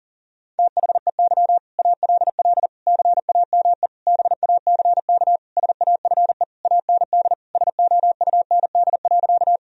Morse code, heard as a keyed tone, about 40 wpm, 700 Hz. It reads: THEY ALL CAME BACK SAFE AND SOUND.
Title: Sentences from Top 1000 Words - Review Part 1 - 40wpm